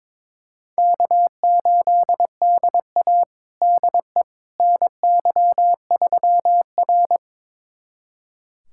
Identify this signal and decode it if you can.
Morse code, heard as a keyed tone, about 22 wpm, 700 Hz. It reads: K8DA DE NY3R
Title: USA Callsign DE Callsign - 4 Characters - 22wpm